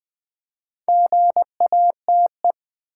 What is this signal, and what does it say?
Morse code, heard as a keyed tone, about 20 wpm, 700 Hz. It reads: GATE
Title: Build A Word - 2 Letters At A Time - 4-Letter Words - 20wpm